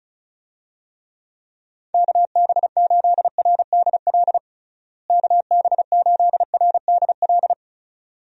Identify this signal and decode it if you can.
Morse code, heard as a keyed tone, about 34 wpm, 700 Hz. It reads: KB8RDL KB8RDL
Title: USA Call Signs - 6 Characters - Repeated 2x - 35wpm